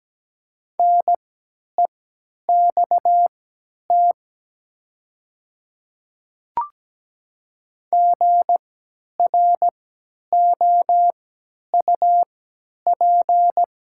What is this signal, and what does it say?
Morse code, heard as a keyed tone, about 17 wpm, spaced as for 10 wpm, 700 Hz. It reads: NEXT GROUP
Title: 3x Character Spacing - Top 300 Words - 17wpm